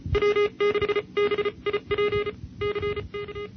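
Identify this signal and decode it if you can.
Morse code, heard as a keyed tone, about 37 wpm, spaced as for 30 wpm, 410 Hz. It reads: W6BIP CK
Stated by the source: Sweepstakes contest Morse recording from a club contest tape.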